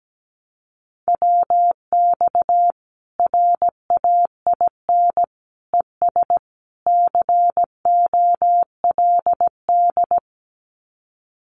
Morse code, heard as a keyed tone, about 17 wpm, 700 Hz. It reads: WX RAIN ES COLD
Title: Common QSO Phrases 17wpm